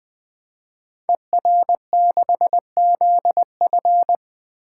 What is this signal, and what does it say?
Morse code, heard as a keyed tone, about 20 wpm, 700 Hz. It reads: ER6ZF
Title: Code Groups - 5 Letter:Numbers - 20wpm